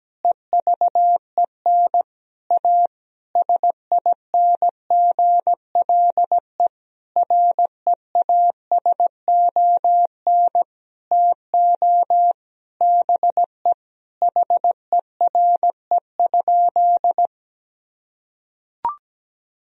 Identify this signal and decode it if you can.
Morse code, heard as a keyed tone, about 17 wpm, 700 Hz. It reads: EVEN A SINGLE REASON TO BE HERE?
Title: Sentences from Top 1000 Words - Review Part 4 - 17wpm